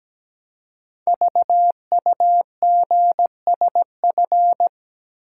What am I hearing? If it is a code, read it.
Morse code, heard as a keyed tone, about 17 wpm, 700 Hz. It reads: VUGSF
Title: Code Groups - 5 Letters - 17wpm